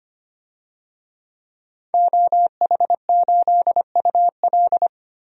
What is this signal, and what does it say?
Morse code, heard as a keyed tone, about 25 wpm, 700 Hz. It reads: OH8UL
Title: DXCC-100 Call Signs - 5 Characters - Repeated 2x - 25wpm